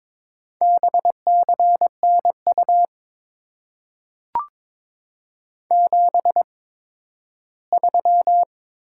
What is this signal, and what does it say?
Morse code, heard as a keyed tone, about 22 wpm, spaced as for 13 wpm, 700 Hz. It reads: BCNU 73
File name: Top 90 QSO Elements - 8x Character Spacing - 22wpm